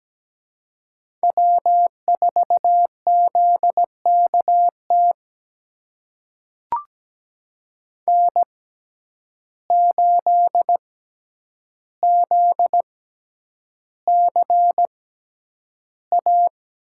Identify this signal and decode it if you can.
Morse code, heard as a keyed tone, about 17 wpm, spaced as for 9 wpm, 700 Hz. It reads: W4ZKT N8ZCA
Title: USA Call Signs - 5 Characters - 6x Character Spacing - 17wpm